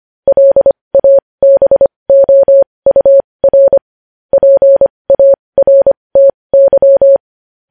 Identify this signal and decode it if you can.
Morse code, heard as a keyed tone, about 25 wpm, spaced as for 20 wpm, 550 Hz. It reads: LABOUR PARTY
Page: News Headlines in Morse Code at 20 WPM